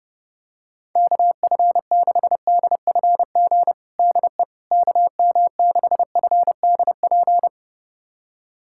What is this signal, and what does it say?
Morse code, heard as a keyed tone, about 30 wpm, 700 Hz. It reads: KF6DFG DE KM6FDP
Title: USA Callsign DE Callsign - 6 Characters - 30wpm